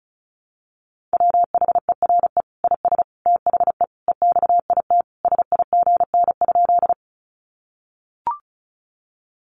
Morse code, heard as a keyed tone, about 34 wpm, 700 Hz. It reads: WHERE IS THE EXIT SIGN?
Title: Sentences from Top 1100 Words 35wpm